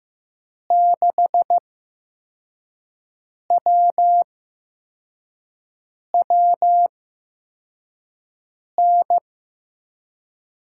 Morse code, heard as a keyed tone, about 15 wpm, spaced as for 4 wpm, 700 Hz. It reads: 6WWN
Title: USA Call Signs - 5 Characters - 8x Character Spacing - 15wpm